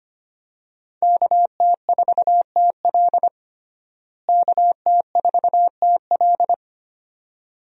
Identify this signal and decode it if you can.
Morse code, heard as a keyed tone, about 25 wpm, 700 Hz. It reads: KT4TL KT4TL
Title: USA Call Signs - 5 Characters - Repeated 2x - 25wpm